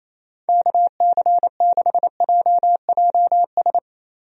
Morse code, heard as a keyed tone, about 28 wpm, 700 Hz. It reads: KC6JJS
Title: USA Call Signs - 6 Characters - Repeated 2x - 28wpm